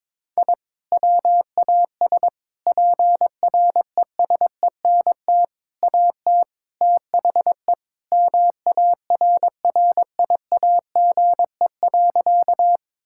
Morse code, heard as a keyed tone, about 22 wpm, 700 Hz. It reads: I WAS PRESENT AT THE MARRIAGE.
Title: Sentences from Top 1000 Words - Review Part 5 - 22wpm